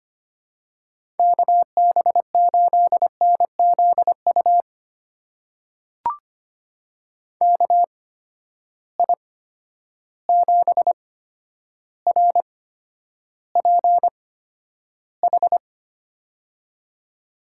Morse code, heard as a keyed tone, about 25 wpm, spaced as for 11 wpm, 700 Hz. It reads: KB8NZU KI7RPH
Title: USA Call Signs - 6 Characters - 8x Character Spacing - 25wpm